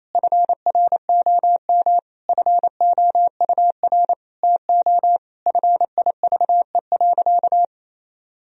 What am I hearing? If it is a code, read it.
Morse code, heard as a keyed tone, about 28 wpm, 700 Hz. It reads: FROM FOUR TO FIVE.
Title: Sentences from Top 1000 Words - Review Part 2 - 28wpm